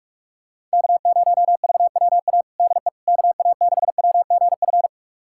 Morse code, heard as a keyed tone, about 44 wpm, 700 Hz. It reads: K0VWA DE KA6WGF
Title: USA Callsign DE Callsign 45wpm